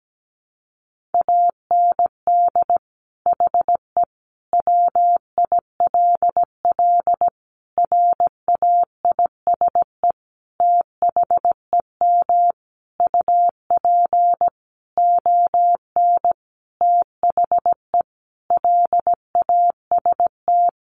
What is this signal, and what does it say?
Morse code, heard as a keyed tone, about 17 wpm, 700 Hz. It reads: AND HE WILL RAISE THEM UP ON THE LAST
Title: Sentences from Top 1400 Words 17wpm